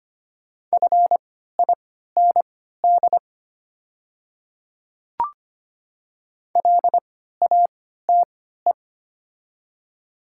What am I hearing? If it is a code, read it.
Morse code, heard as a keyed tone, about 25 wpm, spaced as for 14 wpm, 700 Hz. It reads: FIND LATE
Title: 3x Character Spacing - Top 300 Words - 25wpm